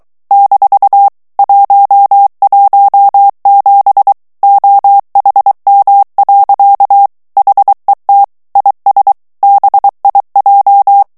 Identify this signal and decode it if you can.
Morse code, heard as a keyed tone, about 23 wpm, 800 Hz. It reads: - 117 OHM. HET IS BIJ
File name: HOLLANDAIS texte clair 1 vitesse 22 mots minute